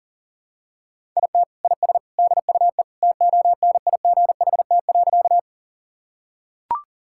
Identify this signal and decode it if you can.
Morse code, heard as a keyed tone, about 37 wpm, 700 Hz. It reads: IT IS DUE TONIGHT.
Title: Sentences from Top 1000 Words - Review Part 6 - 40wpm